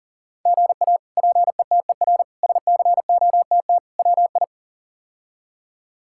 Morse code, heard as a keyed tone, about 40 wpm, 700 Hz. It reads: GA PETER SCOTT WI
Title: K1USN SST QSO Practice - 40wpm